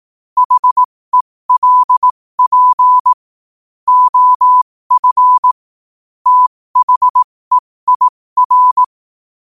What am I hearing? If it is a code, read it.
Morse code, heard as a keyed tone, about 18 wpm, spaced as for 15 wpm, 1000 Hz. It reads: HELP OF THEIR
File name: qotdfiles / 15wpm